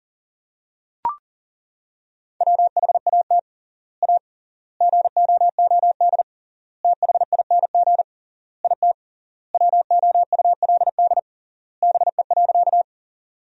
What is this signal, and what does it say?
Morse code, heard as a keyed tone, about 40 wpm, 700 Hz. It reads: WHAT A GOOD THING IT WOULD BE.
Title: 3x Word-Spacing - Sentences from Top 200 Words - 40wpm